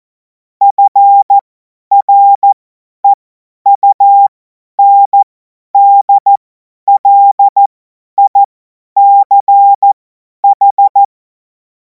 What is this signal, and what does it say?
Morse code, heard as a keyed tone, about 14 wpm, spaced as for 10 wpm, 800 Hz. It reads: FREUNDLICH
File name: Morsekurs / Witze / Witz 14. Juli 2026 50 F70